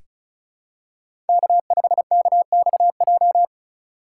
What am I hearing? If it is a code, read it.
Morse code, heard as a keyed tone, about 36 wpm, 700 Hz. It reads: K5KXJ
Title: K1USN SST QSO Practice - 35wpm